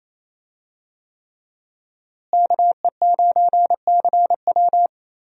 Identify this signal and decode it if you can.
Morse code, heard as a keyed tone, about 27 wpm, 700 Hz. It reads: KE9CW
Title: USA Call Signs - 5 Characters - Repeated 3x - 28wpm